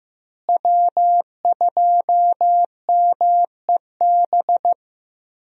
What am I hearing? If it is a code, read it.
Morse code, heard as a keyed tone, about 15 wpm, 700 Hz. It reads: W2MEB
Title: USA Call Signs - 5 Characters - Repeated 2x - 15wpm